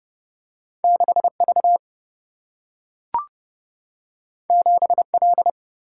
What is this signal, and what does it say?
Morse code, heard as a keyed tone, about 30 wpm, 700 Hz. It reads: 6 V 7 L
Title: Single-Letter Single-Number 30wpm